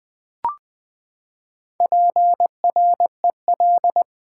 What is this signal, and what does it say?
Morse code, heard as a keyed tone, about 20 wpm, 700 Hz. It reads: PREL
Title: Build A Word - 4 Letters At A Time - 11-Letter Words - 20wpm